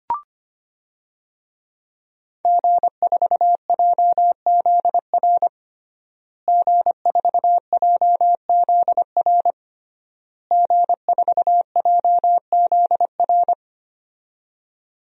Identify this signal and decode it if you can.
Morse code, heard as a keyed tone, about 25 wpm, 700 Hz. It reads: G4JZR G4JZR G4JZR
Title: DXCC-100 Call Signs - 5 Characters - Repeated 3x - 25wpm